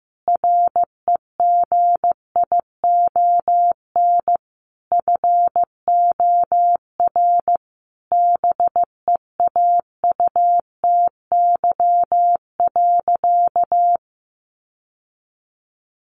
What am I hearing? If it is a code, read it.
Morse code, heard as a keyed tone, about 15 wpm, 700 Hz. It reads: REGION FOR BEAUTY.
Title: Sentences from Top 1500 Words 15wpm